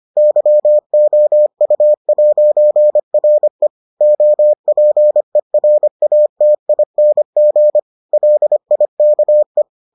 Morse code, heard as a keyed tone, about 25 wpm, 600 Hz. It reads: YOU'RE OPERATING LIKE